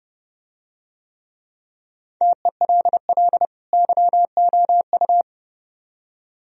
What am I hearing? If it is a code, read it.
Morse code, heard as a keyed tone, about 29 wpm, 700 Hz. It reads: TELL YOU